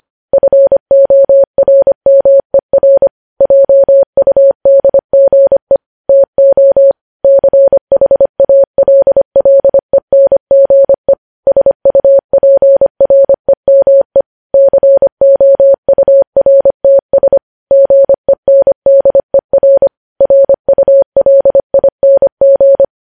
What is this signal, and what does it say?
Morse code, heard as a keyed tone, about 25 wpm, 550 Hz. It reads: FORMER JUDGE TO CHALLENGE SUPREME COURTS GENDER RULING
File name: podcast / News headlines-1x-25x25